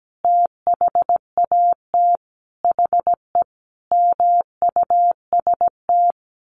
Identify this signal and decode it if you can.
Morse code, heard as a keyed tone, about 17 wpm, 700 Hz. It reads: THAT HE MUST
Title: Sentences from Top 1500 Words 17wpm